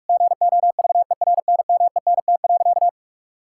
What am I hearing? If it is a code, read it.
Morse code, heard as a keyed tone, about 41 wpm, 700 Hz. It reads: GOVERNMENT.
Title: Sentences from Top 1000 Words - Review Part 6 - 45wpm